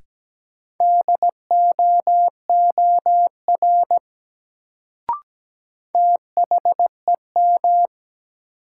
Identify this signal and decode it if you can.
Morse code, heard as a keyed tone, about 17 wpm, 700 Hz. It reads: DOOR THEM
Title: Sets of 2 Words - New Word Drill - 17wpm